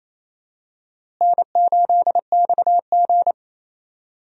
Morse code, heard as a keyed tone, about 28 wpm, 700 Hz. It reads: N8XG
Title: USA Call Signs - 4 Characters - Repeated 3x - 28wpm